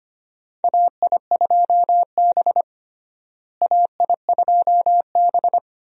Morse code, heard as a keyed tone, about 25 wpm, 700 Hz. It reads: AI2B AI2B
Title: USA Call Signs - 4 Characters - Repeated 2x - 25wpm